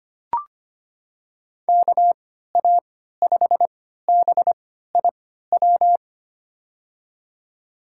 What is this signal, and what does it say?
Morse code, heard as a keyed tone, about 25 wpm, spaced as for 14 wpm, 700 Hz. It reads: KA5BIW
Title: USA Call Signs - 6 Characters - 3x Character Spacing - 25wpm